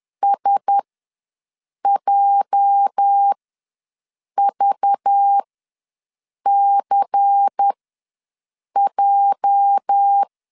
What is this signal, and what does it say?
Morse code, heard as a keyed tone, about 10 wpm, spaced as for 6 wpm, 800 Hz. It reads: SJVCJ
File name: LEZIONI / 8 YJ45 / 8 YJ ws 10wpm 4043